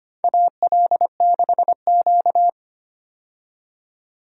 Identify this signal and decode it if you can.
Morse code, heard as a keyed tone, about 25 wpm, 700 Hz. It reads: AL6Q
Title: USA Call Signs - 4 Characters - 25wpm